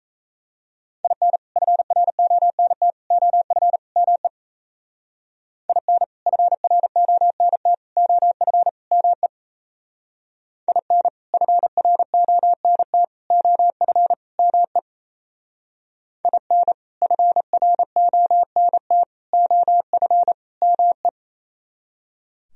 Morse code, heard as a keyed tone, about 34 wpm, 700 Hz. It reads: IN FRONT OF ME IN FRONT OF ME IN FRONT OF ME IN FRONT OF ME
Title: Speed-Racing - Sets of 4 Words 28wpm